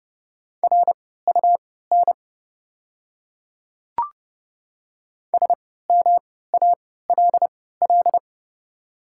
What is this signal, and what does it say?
Morse code, heard as a keyed tone, about 30 wpm, spaced as for 17 wpm, 690 Hz. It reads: RUN SMALL
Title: 3x Character Spacing - Top 300 Words - 30wpm